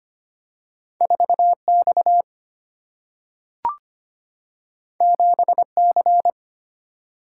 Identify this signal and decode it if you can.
Morse code, heard as a keyed tone, about 25 wpm, 700 Hz. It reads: 4X 7C
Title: Single-Letter Single-Number 25wpm